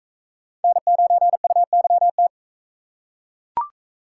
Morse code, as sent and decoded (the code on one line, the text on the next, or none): -. ----. ..- -.-- -
N9UYT